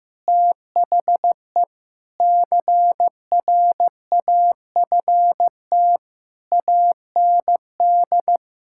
Morse code, heard as a keyed tone, about 15 wpm, 700 Hz. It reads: THE CRAFT AND
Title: Sentences from Top 1000 Words - Review Part 6 - 15wpm